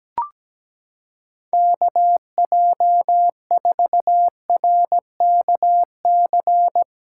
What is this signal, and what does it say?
Morse code, heard as a keyed tone, about 17 wpm, 700 Hz. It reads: KJ4RKC
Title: USA Callsign DE Callsign 17wpm